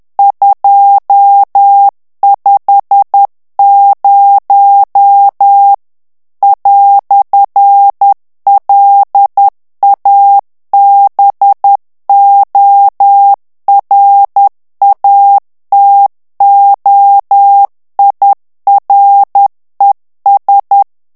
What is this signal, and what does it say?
Morse code, heard as a keyed tone, about 10 wpm, 800 Hz. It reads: 250 "LABORATOIRES
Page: Français - texte en clair 1 - vitesse : 10 mots minute